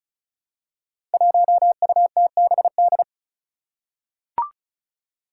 Morse code, heard as a keyed tone, about 36 wpm, 700 Hz. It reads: 1UTBD
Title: Five Challenging Letter-Number Combinations 35wpm